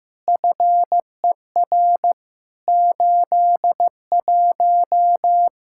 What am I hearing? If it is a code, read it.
Morse code, heard as a keyed tone, about 15 wpm, 700 Hz. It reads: FER 81
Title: Practice QSO - Exchange 3 - More Info - 15wpm